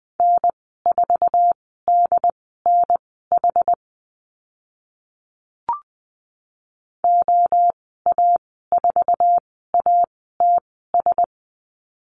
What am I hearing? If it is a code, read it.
Morse code, heard as a keyed tone, about 20 wpm, spaced as for 14 wpm, 700 Hz. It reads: N4DNH OA4ATS
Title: 069 Lesson 12g - Letter d - Call Signs 20wpm